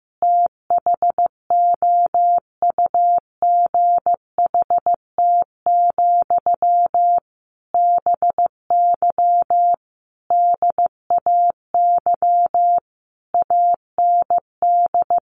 Morse code, heard as a keyed tone, about 14 wpm, 700 Hz. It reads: THOUGHT, BY DAY AND
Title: Sentences from Top 1500 Words 15wpm